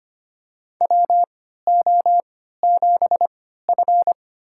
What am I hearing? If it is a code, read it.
Morse code, heard as a keyed tone, about 25 wpm, spaced as for 14 wpm, 700 Hz. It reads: WO7F
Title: USA Call Signs - 4 Characters - 3x Character Spacing - 25wpm